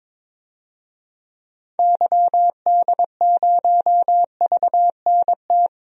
Morse code, heard as a keyed tone, about 22 wpm, 700 Hz. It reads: YD0VNT
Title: DXCC-100 Call Signs - 6 Characters - 22wpm